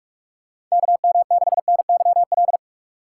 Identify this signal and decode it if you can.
Morse code, heard as a keyed tone, about 46 wpm, 700 Hz. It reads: KM6NYL
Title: USA Call Signs - 6 Characters - 3x Character Spacing - 45wpm